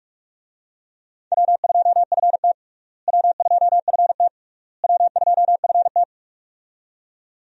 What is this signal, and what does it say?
Morse code, heard as a keyed tone, about 46 wpm, 700 Hz. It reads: W2FT W2FT W2FT
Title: USA Call Signs - 4 Characters - Repeated 3x - 45wpm